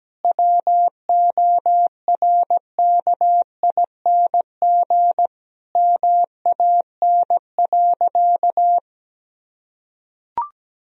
Morse code, heard as a keyed tone, about 17 wpm, 700 Hz. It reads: WORKING MAN.
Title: Sentences from Top 1000 Words - Review Part 4 - 17wpm